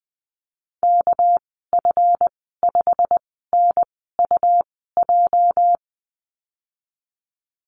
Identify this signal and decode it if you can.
Morse code, heard as a keyed tone, about 20 wpm, spaced as for 15 wpm, 700 Hz. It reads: KF5NUJ
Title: 213 Lesson 34g - Number 8 - Call Signs 20wpm